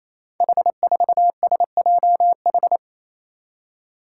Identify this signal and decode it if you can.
Morse code, heard as a keyed tone, about 27 wpm, 700 Hz. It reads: H4SJH
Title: Five Challenging Letter-Number Combinations 28wpm